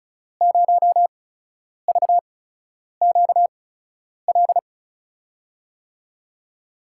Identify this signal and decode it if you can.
Morse code, heard as a keyed tone, about 36 wpm, spaced as for 10 wpm, 700 Hz. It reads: 0VQL
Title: USA Call Signs - 6 Characters - 8x Character Spacing - 35wpm